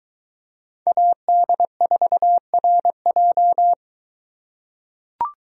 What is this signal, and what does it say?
Morse code, heard as a keyed tone, about 23 wpm, 700 Hz. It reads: AD4RJ